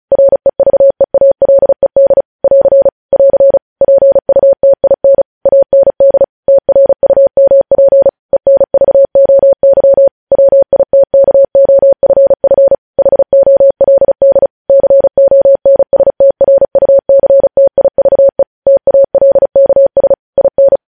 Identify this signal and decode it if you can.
Morse code, heard as a keyed tone, about 36 wpm, 550 Hz. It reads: REVEALED <AR> <AR> PUTIN AND TRUMP ENVOY WITKOFF HOLD CONSTRUCTIVE TALKS IN
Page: News headlines in Morse code at 35 wpm